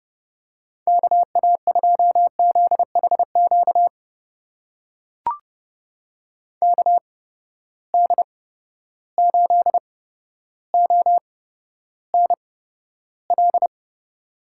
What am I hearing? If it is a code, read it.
Morse code, heard as a keyed tone, about 30 wpm, spaced as for 13 wpm, 700 Hz. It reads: KA2ZHQ KD8ONL
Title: USA Call Signs - 6 Characters - 8x Character Spacing - 30wpm